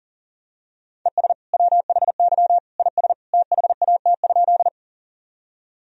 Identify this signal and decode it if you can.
Morse code, heard as a keyed tone, about 40 wpm, 700 Hz. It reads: ES WHY IS THAT?
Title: Sentences from Top 200 Words and Abbreviations 40wpm